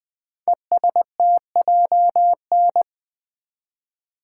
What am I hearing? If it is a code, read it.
Morse code, heard as a keyed tone, about 20 wpm, 700 Hz. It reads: ESTJN
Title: Code Groups - 5 Letter:Numbers - 20wpm